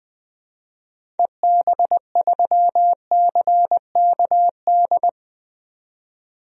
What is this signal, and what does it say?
Morse code, heard as a keyed tone, about 20 wpm, 700 Hz. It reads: EB3CKD
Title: DXCC-100 Call Signs - 6 Characters - 20wpm